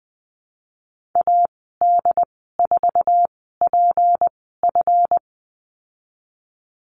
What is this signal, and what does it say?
Morse code, heard as a keyed tone, about 20 wpm, spaced as for 15 wpm, 700 Hz. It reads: AD4PF
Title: 179 Lesson 29g - Slash - Call Signs 20wpm